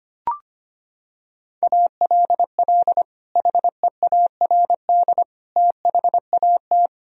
Morse code, heard as a keyed tone, about 25 wpm, 700 Hz. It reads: ALL HEARD THAT
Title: Sentences from Top 1000 Words - Review Part 1 - 25wpm